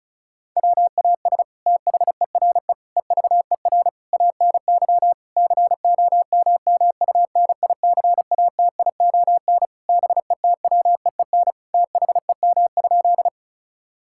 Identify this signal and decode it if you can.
Morse code, heard as a keyed tone, about 36 wpm, 700 Hz. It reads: WAS THERE EVER ANY COMMUNICATION BETWEEN THEM?
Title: Sentences from Top 1000 Words - Review Part 6 - 35wpm